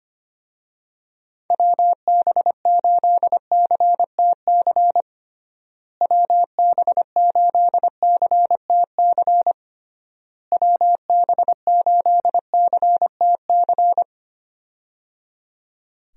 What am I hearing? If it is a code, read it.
Morse code, heard as a keyed tone, about 25 wpm, 700 Hz. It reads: WB8CTC WB8CTC WB8CTC
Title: USA Call Signs - 6 Characters - Repeated 3x - 25wpm